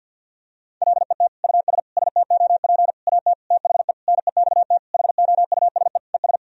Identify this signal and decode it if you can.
Morse code, heard as a keyed tone, about 49 wpm, 700 Hz. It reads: LET US STOP AT THE NEXT HOUSE ES
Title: Sentences from Top 400 Words and Abbreviations 50wpm